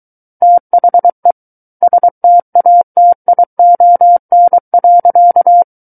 Morse code, heard as a keyed tone, about 23 wpm, 700 Hz. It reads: THE STATION.
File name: Book One / 14-Chapter 14 - 23 WPM